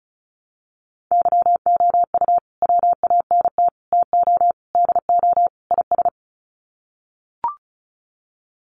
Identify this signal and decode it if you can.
Morse code, heard as a keyed tone, about 34 wpm, 700 Hz. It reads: YOU WANT TO DO IS
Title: Sets of 5 Words Encore2 35wpm